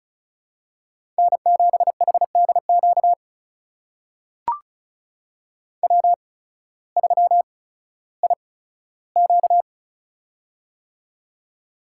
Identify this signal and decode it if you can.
Morse code, heard as a keyed tone, about 36 wpm, spaced as for 16 wpm, 700 Hz. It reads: N7HDQ W3IQ
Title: USA Call Signs - 8x Character Spacing - 35wpm